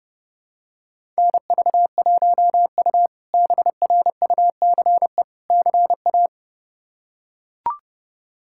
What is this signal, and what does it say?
Morse code, heard as a keyed tone, about 30 wpm, 700 Hz. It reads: NV1U BRUCE CA
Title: K1USN SST QSO Practice - 30wpm